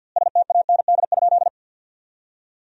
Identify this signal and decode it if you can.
Morse code, heard as a keyed tone, about 52 wpm, 700 Hz. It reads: STAND?
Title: 3x Word-Spacing - Sentences from Top 300 Words - 50wpm